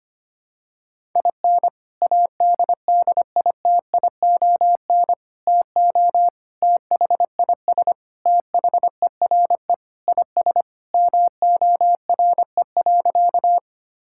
Morse code, heard as a keyed tone, about 25 wpm, 700 Hz. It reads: IN ADDITION TO THIS THERE IS MORE.
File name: practice / 25wpm / Sentences with 3-Word Phrases - 25wpm